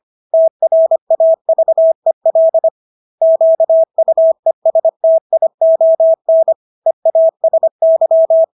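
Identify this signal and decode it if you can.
Morse code, heard as a keyed tone, about 25 wpm, 650 Hz. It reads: TRAVEL QUESTION EASY
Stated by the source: Morse practice of English words at 25 wpm from CW med Gnister.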